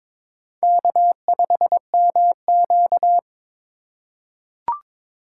Morse code, heard as a keyed tone, about 21 wpm, 700 Hz. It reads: K5MQ
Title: USA Call Signs - 4 Characters - 2x Character Spacing - 22wpm